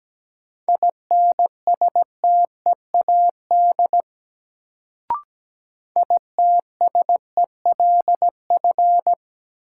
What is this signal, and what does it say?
Morse code, heard as a keyed tone, about 17 wpm, 700 Hz. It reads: INSTEAD ITSELF
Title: Sentences from Top 800 Words - New Word Drill - 17wpm